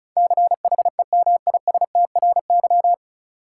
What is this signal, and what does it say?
Morse code, heard as a keyed tone, about 36 wpm, 700 Hz. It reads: CHEMISTRY
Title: Sentences from Top 1000 Words - Review Part 6 - New Word Drill - 35wpm